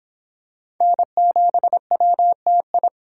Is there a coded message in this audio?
Yes: Morse code from 0.8 to 2.9 s, heard as a keyed tone, about 25 wpm, 700 Hz. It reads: N7WTI